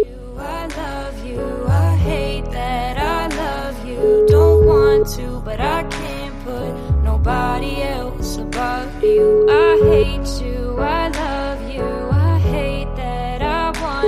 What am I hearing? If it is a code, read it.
Ringback tone (Europe, 425 Hz), with noise.